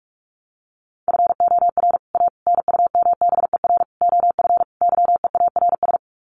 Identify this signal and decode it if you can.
Morse code, heard as a keyed tone, about 44 wpm, 700 Hz. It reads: FOR A NUMBER OF YEARS
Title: Sets of 5 Words Encore 45wpm